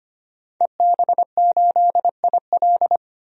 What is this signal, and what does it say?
Morse code, heard as a keyed tone, about 25 wpm, 700 Hz. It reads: EB8IL
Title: DXCC-100 Call Signs - 5 Characters - 25wpm